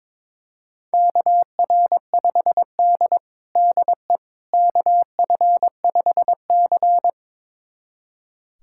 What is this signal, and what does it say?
Morse code, heard as a keyed tone, about 21 wpm, 700 Hz. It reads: KR5D DE KF5C
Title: USA Callsign DE Callsign - 4 Characters - 22wpm